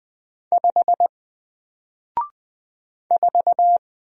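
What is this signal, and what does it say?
Morse code, heard as a keyed tone, about 20 wpm, 700 Hz. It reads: 5 4
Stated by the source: Single letter and number Morse drill at 20 wpm, from N4LCD.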